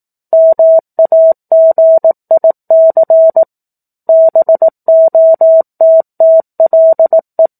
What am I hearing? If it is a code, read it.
Morse code, heard as a keyed tone, about 18 wpm, 650 Hz. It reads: MAGIC BOTTLE
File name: 04-Chapter-04 18 WPM